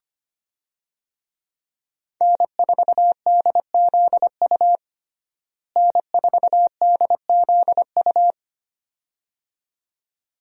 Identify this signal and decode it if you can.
Morse code, heard as a keyed tone, about 25 wpm, 700 Hz. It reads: N4DZU N4DZU
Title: USA Call Signs - 5 Characters - Repeated 2x - 25wpm